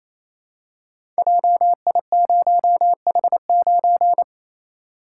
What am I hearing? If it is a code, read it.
Morse code, heard as a keyed tone, about 28 wpm, 700 Hz. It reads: JI0H9
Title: Code Groups - 5 Letter:Numbers - 28wpm